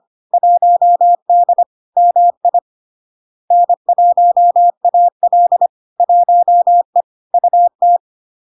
Morse code, heard as a keyed tone, about 25 wpm, 700 Hz. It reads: 1D MI N1AL 1E UT
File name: Log-in-CW-25wpm-700-Hz